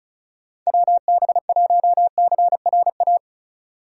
Morse code, heard as a keyed tone, about 36 wpm, 700 Hz. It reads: WB1CRA
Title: USA Call Signs - 6 Characters - 8x Character Spacing - 35wpm